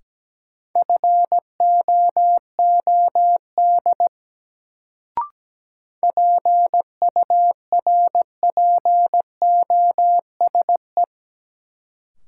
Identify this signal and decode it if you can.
Morse code, heard as a keyed tone, about 17 wpm, 700 Hz. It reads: FOOD PURPOSE
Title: Sets of 5 Words - New Word Drill - 17wpm